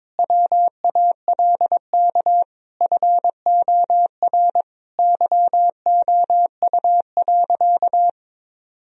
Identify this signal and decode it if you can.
Morse code, heard as a keyed tone, about 22 wpm, 700 Hz. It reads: WALK FOR YOU.
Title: Sentences from Top 1000 Words - Review Part 1 - 22wpm